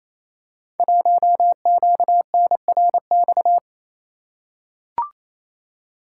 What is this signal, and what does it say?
Morse code, heard as a keyed tone, about 29 wpm, 690 Hz. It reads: 1QNRX
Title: Code Groups - 5 Letter:Numbers - 28wpm